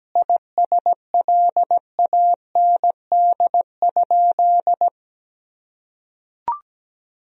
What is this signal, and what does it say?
Morse code, heard as a keyed tone, about 17 wpm, 700 Hz. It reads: ISLAND?
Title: Sentences from Top 1000 Words - Review Part 3 - 17wpm